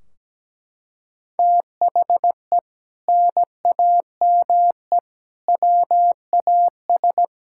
Morse code, heard as a keyed tone, about 17 wpm, 700 Hz. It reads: THE NAME WAS
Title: Sentences from Top 1600 Words 17wpm